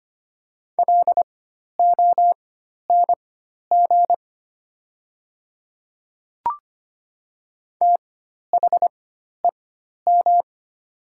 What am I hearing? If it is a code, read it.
Morse code, heard as a keyed tone, about 25 wpm, spaced as for 12 wpm, 700 Hz. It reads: LONG THEM
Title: 4x Character Spacing - Top 100 Words - 25wpm